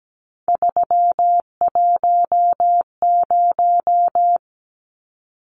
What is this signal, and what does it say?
Morse code, heard as a keyed tone, about 17 wpm, 700 Hz. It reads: 310
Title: Numbers - 3-Digit 17wpm